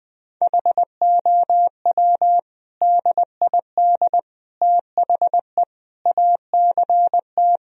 Morse code, heard as a keyed tone, about 21 wpm, 700 Hz. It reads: HOW DID THE ACT
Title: Sentences from Top 1000 Words - Review Part 4 - 20wpm